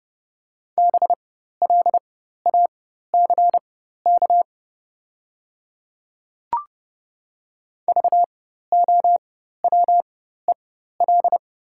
Morse code, heard as a keyed tone, about 30 wpm, spaced as for 14 wpm, 700 Hz. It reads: BLACK VOWEL
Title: 4x Character Spacing - Top 500 Words - 30wpm